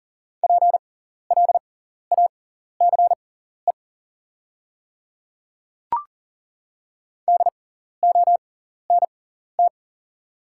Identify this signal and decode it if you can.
Morse code, heard as a keyed tone, about 40 wpm, spaced as for 14 wpm, 700 Hz. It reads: PLACE DONT
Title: 6x Character Spacing - Top 300 Words - 40wpm